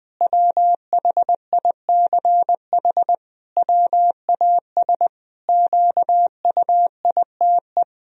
Morse code, heard as a keyed tone, about 20 wpm, 700 Hz. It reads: WHICH WAS QUITE
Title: Sentences from Top 1000 Words - Review Part 3 - 20wpm